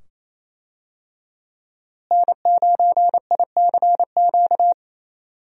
Morse code, heard as a keyed tone, about 28 wpm, 700 Hz. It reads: N9ICQ
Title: USA Call Signs - 5 Characters - Repeated 2x - 28wpm